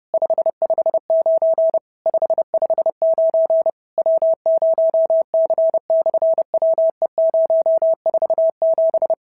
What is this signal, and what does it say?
Morse code, heard as a keyed tone, about 30 wpm, 650 Hz. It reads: 559 559 W0C/WE047
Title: SOTA QSO Practice 30wpm